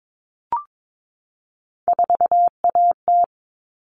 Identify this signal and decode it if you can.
Morse code, heard as a keyed tone, about 22 wpm, 700 Hz. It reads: 4AT
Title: All Cut Numbers - 3-Digit 22wpm